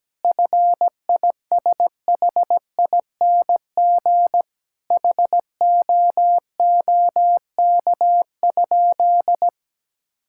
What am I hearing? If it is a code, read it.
Morse code, heard as a keyed tone, about 17 wpm, 700 Hz. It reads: FISHING HOOK?
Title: Sentences from Top 1600 Words 17wpm